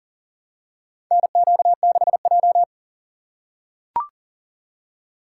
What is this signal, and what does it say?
Morse code, heard as a keyed tone, about 40 wpm, 700 Hz. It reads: NQ6J
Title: USA Call Signs - 4 Characters - 3x Character Spacing - 40wpm